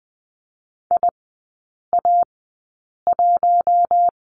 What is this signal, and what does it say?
Morse code, heard as a keyed tone, about 20 wpm, spaced as for 8 wpm, 700 Hz. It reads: IA1
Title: 041 Lesson 8e - Number 1 - Three Characters 20wpm